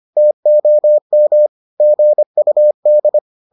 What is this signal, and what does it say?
Morse code, heard as a keyed tone, about 25 wpm, 600 Hz. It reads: TOM GUD